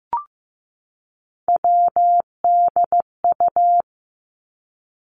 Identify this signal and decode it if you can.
Morse code, heard as a keyed tone, about 16 wpm, 700 Hz. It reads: WDU
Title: Three Challenging Letter-Number Combinations 15wpm